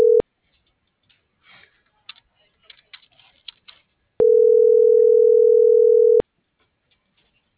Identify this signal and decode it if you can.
Ringback tone (North America), 440 + 480 Hz.
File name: CENTRAL BLN / out ximena.diaz